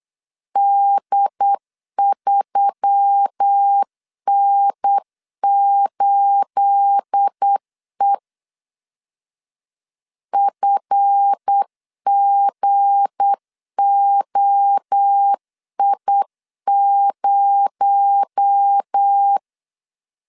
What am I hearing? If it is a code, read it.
Morse code, heard as a keyed tone, about 8 wpm, 800 Hz. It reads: D3N8E FGOI0
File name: LEZIONI / 9 28 8wpm 3512